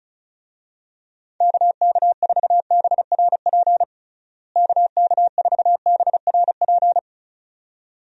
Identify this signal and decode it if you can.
Morse code, heard as a keyed tone, about 34 wpm, 700 Hz. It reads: KK4BRP KK4BRP
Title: USA Call Signs - 6 Characters - Repeated 2x - 35wpm